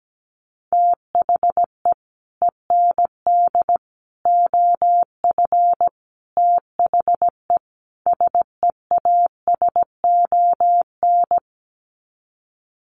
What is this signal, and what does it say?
Morse code, heard as a keyed tone, about 17 wpm, 700 Hz. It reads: THE END OF THE SEASON
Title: Sets of 5 Words Encore 17wpm